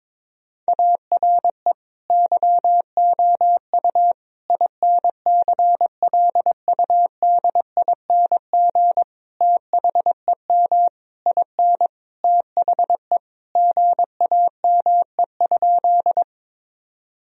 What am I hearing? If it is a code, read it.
Morse code, heard as a keyed tone, about 21 wpm, 700 Hz. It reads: ARE YOU INCLUDING THEM IN THE GAME?
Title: Sentences from Top 1000 Words - Review Part 4 - 22wpm